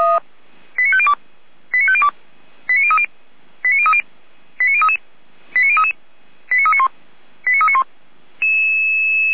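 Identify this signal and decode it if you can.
Five-tone selective call (ZVEI), heard as quick runs of five tones, tones of about 80 ms. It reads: 89502 89502 89030 89030 89033 89033 89391 89391
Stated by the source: ZVEI five-tone selective call received on SDR.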